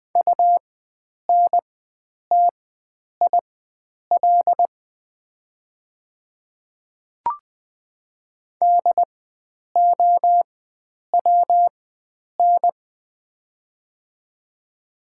Morse code, heard as a keyed tone, about 20 wpm, spaced as for 9 wpm, 700 Hz. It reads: UNTIL DOWN
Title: 4x Character Spacing - Top 500 Words - 20wpm